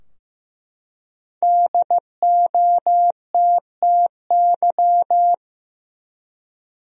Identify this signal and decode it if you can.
Morse code, heard as a keyed tone, about 16 wpm, 700 Hz. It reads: DOTTY
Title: USA Common Names - Max 5 Letters - 15wpm